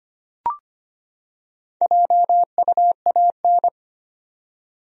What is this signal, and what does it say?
Morse code, heard as a keyed tone, about 25 wpm, 700 Hz. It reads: JUAN
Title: USA Common Names - Max 4 Letters - 25wpm